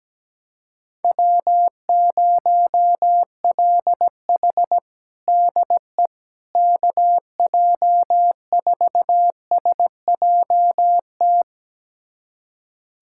Morse code, heard as a keyed tone, about 17 wpm, 700 Hz. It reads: W0LH DE KJ4SJT
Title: USA Callsign DE Callsign 17wpm